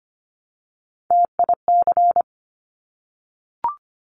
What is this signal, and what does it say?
Morse code, heard as a keyed tone, about 25 wpm, 700 Hz. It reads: TIC
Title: Top Three Letter Combinations in Words 25wpm